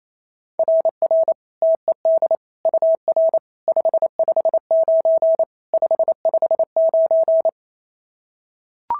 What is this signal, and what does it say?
Morse code, heard as a keyed tone, about 28 wpm, 690 Hz. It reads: RR TED UR 559 559
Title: SOTA QSO Practice 28wpm